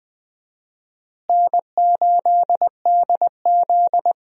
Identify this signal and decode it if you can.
Morse code, heard as a keyed tone, about 21 wpm, 700 Hz. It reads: N8DZ
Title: USA Call Signs - 4 Characters - Repeated 3x - 20wpm